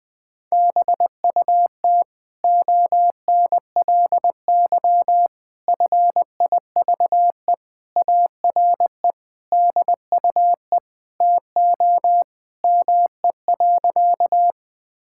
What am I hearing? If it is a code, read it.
Morse code, heard as a keyed tone, about 20 wpm, 700 Hz. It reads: BUT ONLY FIVE ARE DUE TO ME.
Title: Sentences from Top 1000 Words - Review Part 6 - 20wpm